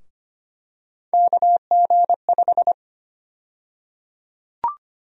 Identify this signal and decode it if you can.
Morse code, heard as a keyed tone, about 25 wpm, 700 Hz. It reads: KG5
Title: USA Call Sign Prefixes - 25wpm